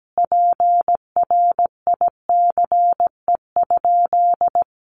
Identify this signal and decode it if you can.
Morse code, heard as a keyed tone, about 17 wpm, 700 Hz. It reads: PRICE?
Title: Sentences from Top 1400 Words 17wpm